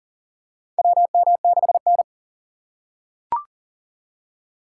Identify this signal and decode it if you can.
Morse code, heard as a keyed tone, about 40 wpm, 690 Hz. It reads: WM6N
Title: USA Call Signs - 4 Characters - 2x Character Spacing - 40wpm